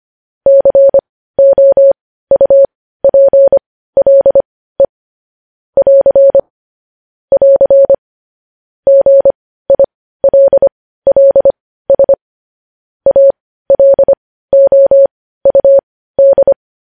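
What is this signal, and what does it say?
Morse code, heard as a keyed tone, about 25 wpm, spaced as for 15 wpm, 550 Hz. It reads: COUPLE <AR> <AR> GILLS ALOUD
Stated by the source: News headlines sent in Morse code at 15 wpm.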